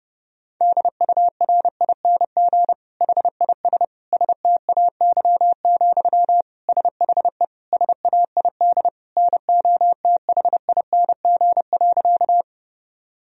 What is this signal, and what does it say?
Morse code, heard as a keyed tone, about 32 wpm, 700 Hz. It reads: DURING HIS STAY, SHE SAID NOTHING.
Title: Sentences from Top 1000 Words - Review Part 2 - 30wpm